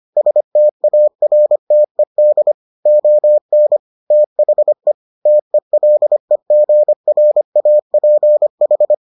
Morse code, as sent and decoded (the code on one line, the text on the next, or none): ... - .- .-. - . -.. / --- -. / - .... / - . .-.. --. .-. .- .--. ....
STARTED ON THE TELEGRAPH